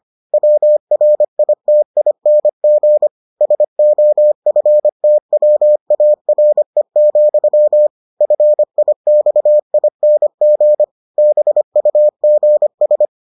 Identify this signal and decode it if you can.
Morse code, heard as a keyed tone, about 25 wpm, 600 Hz. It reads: WRITING SOFTWARE, FIXING BUGS